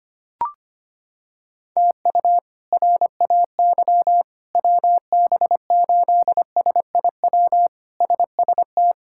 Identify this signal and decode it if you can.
Morse code, heard as a keyed tone, about 25 wpm, 700 Hz. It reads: TU RAY WB8SIW SST
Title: K1USN SST QSO Practice - 25wpm